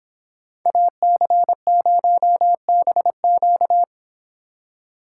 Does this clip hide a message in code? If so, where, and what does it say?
Yes: Morse code from 0.7 to 3.8 s, heard as a keyed tone, about 26 wpm, 700 Hz. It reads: AC0BQ